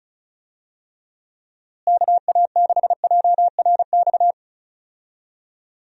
Morse code, heard as a keyed tone, about 36 wpm, 700 Hz. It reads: KA6JRX
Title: USA Call Signs - 6 Characters - Repeated 3x - 35wpm